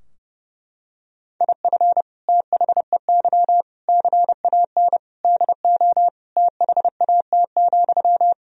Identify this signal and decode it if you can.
Morse code, heard as a keyed tone, about 30 wpm, 700 Hz. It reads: IF THEY CAN DO THAT,
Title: Sentences from Top 1000 Words - Review Part 4 - 30wpm